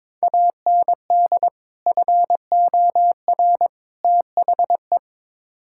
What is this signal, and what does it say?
Morse code, heard as a keyed tone, about 22 wpm, 700 Hz. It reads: AND FOR THE
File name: practice / 22wpm / Build A Phrase - 5-Word Phrases - 22wpm